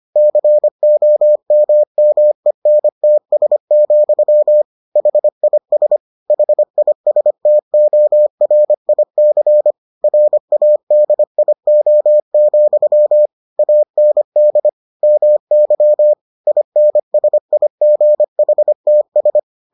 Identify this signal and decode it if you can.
Morse code, heard as a keyed tone, about 25 wpm, 600 Hz. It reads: COMMENTS, HIS HISTORIC RADIO, AND MY INSIGHTS